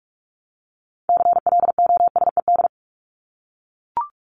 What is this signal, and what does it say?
Morse code, heard as a keyed tone, about 44 wpm, 700 Hz. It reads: CLOSED
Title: Top 1251-1500 Words 45wpm